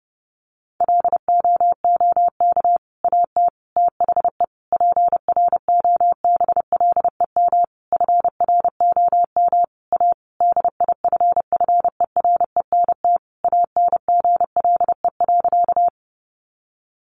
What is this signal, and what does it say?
Morse code, heard as a keyed tone, about 28 wpm, 700 Hz. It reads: LOOK AT THE PROBLEM FROM A DIFFERENT ANGLE.
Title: Sentences from Top 1500 Words 30wpm